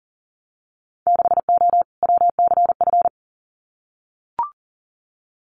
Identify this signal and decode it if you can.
Morse code, heard as a keyed tone, about 42 wpm, 700 Hz. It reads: 6O WCF
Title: Winter Field Day Exchanges Encore 40wpm